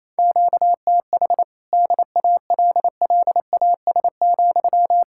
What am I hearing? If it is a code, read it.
Morse code, heard as a keyed tone, about 28 wpm, 700 Hz. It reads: QTH DALLAS,
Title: Practice QSO - Exchange 1 - Beginning - 28wpm